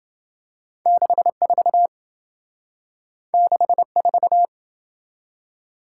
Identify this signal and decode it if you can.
Morse code, heard as a keyed tone, about 28 wpm, 700 Hz. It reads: 64 64